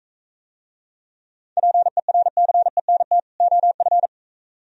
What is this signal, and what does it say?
Morse code, heard as a keyed tone, about 43 wpm, 700 Hz. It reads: PERCENT OF